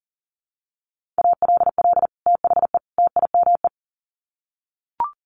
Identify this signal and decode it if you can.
Morse code, heard as a keyed tone, about 40 wpm, 700 Hz. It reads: ALL THE TIME
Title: Sets of 3 Words 40wpm